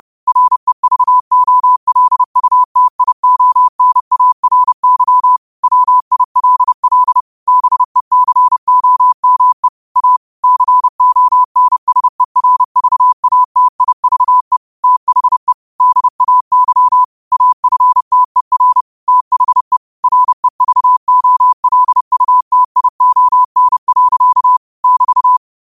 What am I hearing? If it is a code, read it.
Morse code, heard as a keyed tone, about 30 wpm, 1000 Hz. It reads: REVOLUTIONARY WILL BECOME A CONSERVATIVE THE DAY AFTER THE REVOLUTION. <BT>
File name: qotdfiles / 30wpm